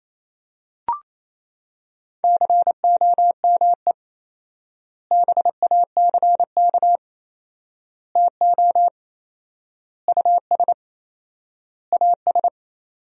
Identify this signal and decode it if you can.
Morse code, heard as a keyed tone, about 28 wpm, 700 Hz. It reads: COME BACK TO US AS
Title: 4x Word-Spacing - Sentences from Top 500 Words - 28wpm